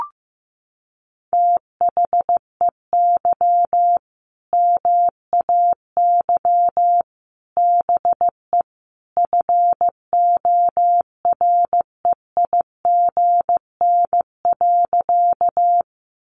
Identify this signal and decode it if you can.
Morse code, heard as a keyed tone, about 15 wpm, 700 Hz. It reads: THEY MAY BE FOREIGN.
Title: Sentences from Top 1400 Words 15wpm